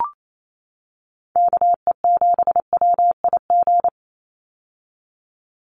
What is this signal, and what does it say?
Morse code, heard as a keyed tone, about 28 wpm, 700 Hz. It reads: KE7WIG
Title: USA Call Signs 28wpm